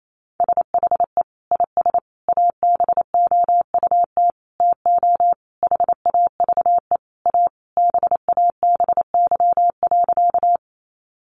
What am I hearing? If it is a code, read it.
Morse code, heard as a keyed tone, about 28 wpm, 700 Hz. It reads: SHE IS ABOUT TO HAVE A BABY.
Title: Sentences from Top 1300 Words 28wpm